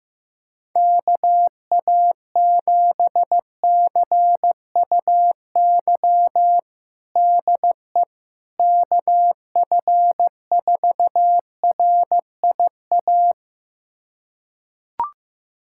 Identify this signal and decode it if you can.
Morse code, heard as a keyed tone, about 15 wpm, 700 Hz. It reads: KA7CUY DE KF4RIA
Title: USA Callsign DE Callsign 15wpm